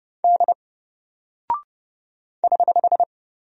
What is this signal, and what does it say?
Morse code, heard as a keyed tone, about 30 wpm, 700 Hz. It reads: D <HH>
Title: Single Character (Letter:Number:Punctuation:Prosign) 30wpm